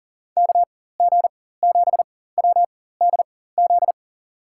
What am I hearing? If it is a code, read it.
Morse code, heard as a keyed tone, about 40 wpm, spaced as for 19 wpm, 700 Hz. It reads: KG7WDZ
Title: USA Call Signs - 6 Characters - 4x Character Spacing - 40wpm